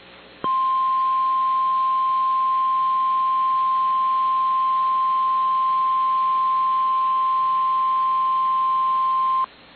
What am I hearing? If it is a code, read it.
Emergency Alert System (SAME) tones (the 1050 Hz attention tone). It is heard from 0.6 to 9.3 s.
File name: EAS Logs / KMNR